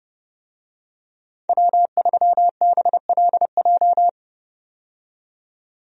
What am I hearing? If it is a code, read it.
Morse code, heard as a keyed tone, about 32 wpm, 700 Hz. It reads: W3BLJ
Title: USA Call Signs - 5 Characters - 30wpm